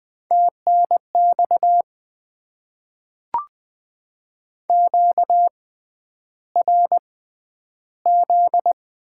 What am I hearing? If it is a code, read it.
Morse code, heard as a keyed tone, about 20 wpm, spaced as for 10 wpm, 700 Hz. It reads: TNX QRZ
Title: Top 90 QSO Elements - 6x Character Spacing - 20wpm